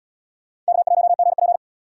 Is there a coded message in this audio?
Yes: Morse code from 0.7 to 1.6 s, heard as a keyed tone, about 67 wpm, 700 Hz. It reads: K1GP